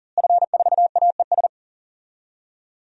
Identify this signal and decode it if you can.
Morse code, heard as a keyed tone, about 40 wpm, 700 Hz. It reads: F4AES
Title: DXCC-100 Call Signs - 5 Characters - Repeated 2x - 40wpm